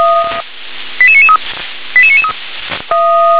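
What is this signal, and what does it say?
Five-tone selective call (ZVEI), heard as quick runs of five tones, tones of about 70 ms. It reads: 88993 88993
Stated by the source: ZVEI five-tone call received by SDR.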